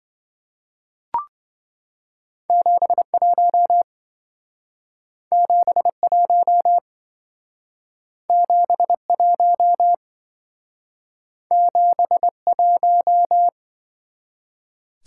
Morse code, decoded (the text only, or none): 71 71 71 71